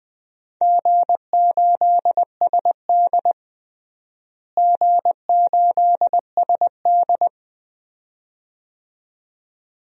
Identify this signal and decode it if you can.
Morse code, heard as a keyed tone, about 20 wpm, 700 Hz. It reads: G8SD G8SD
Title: DXCC-100 Call Signs - 4 Characters - Repeated 2x - 20wpm